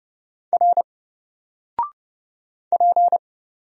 Morse code, heard as a keyed tone, about 30 wpm, 700 Hz. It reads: R P
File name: Single-Letter-Number-30wpm